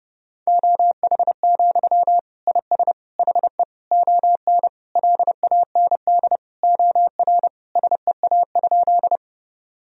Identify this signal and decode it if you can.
Morse code, heard as a keyed tone, about 30 wpm, 700 Hz. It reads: OH, IS HE ON LAND OR SEA?
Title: Sentences from Top 1000 Words - Review Part 1 - 30wpm